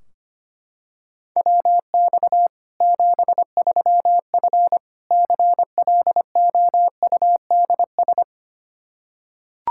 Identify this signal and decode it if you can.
Morse code, heard as a keyed tone, about 25 wpm, 700 Hz. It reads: WX 73F CLOUDS
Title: Practice QSO - Exchange 2 - Info - 25wpm